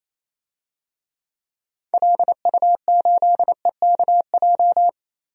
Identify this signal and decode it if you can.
Morse code, heard as a keyed tone, about 27 wpm, 700 Hz. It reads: LU8EKJ
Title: DXCC-100 Call Signs - 6 Characters - Repeated 3x - 28wpm